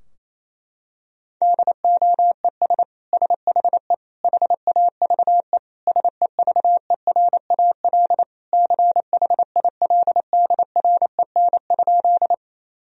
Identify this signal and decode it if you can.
Morse code, heard as a keyed tone, about 28 wpm, 700 Hz. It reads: DOES SHE HAVE SEVERAL CHILDREN?
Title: Sentences from Top 1000 Words - Review Part 2 - 28wpm